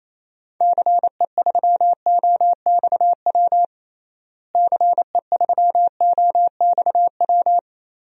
Morse code, heard as a keyed tone, about 28 wpm, 700 Hz. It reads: CE3OXW CE3OXW
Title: DXCC-100 Call Signs - 6 Characters - Repeated 3x - 28wpm